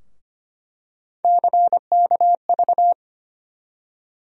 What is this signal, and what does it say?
Morse code, heard as a keyed tone, about 25 wpm, 700 Hz. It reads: CKV
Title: USA Call Sign Postfixes - 3 Letters 25wpm